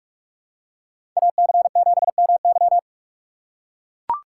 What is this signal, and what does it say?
Morse code, heard as a keyed tone, about 44 wpm, 690 Hz. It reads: AC7MY
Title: USA Call Signs - 5 Characters - 6x Character Spacing - 45wpm